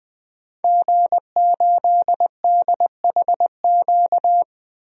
Morse code, heard as a keyed tone, about 20 wpm, 700 Hz. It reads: G8DHQ
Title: DXCC-100 Call Signs - 5 Characters - 20wpm